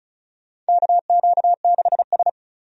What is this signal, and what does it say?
Morse code, heard as a keyed tone, about 34 wpm, 700 Hz. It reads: KQ6S
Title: USA Call Signs - 4 Characters - 35wpm